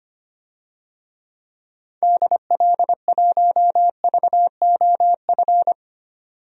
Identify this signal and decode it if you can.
Morse code, heard as a keyed tone, about 25 wpm, 700 Hz. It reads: DL1VOF
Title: DXCC-100 Call Signs - 6 Characters - 25wpm